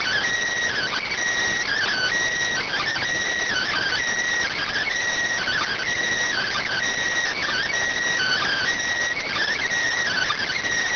Emergency Alert System (SAME) tones (header bursts), heard as harsh buzzing bursts. It is heard from 0.2 to 9.8 s.